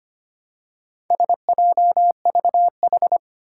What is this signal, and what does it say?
Morse code, heard as a keyed tone, about 25 wpm, 700 Hz. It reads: SJVH
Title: Four Challenging Letter-Number Combinations 25wpm